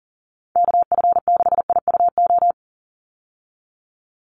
Morse code, heard as a keyed tone, about 40 wpm, 700 Hz. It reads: KF6IUO
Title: USA Call Signs 40wpm